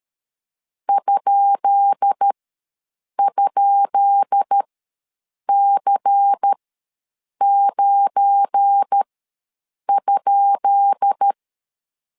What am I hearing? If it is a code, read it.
Morse code, heard as a keyed tone, about 13 wpm, spaced as for 7 wpm, 800 Hz. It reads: ??C9?
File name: LEZIONI / 10 SIGNS / 10B INT ws 12wpm 3527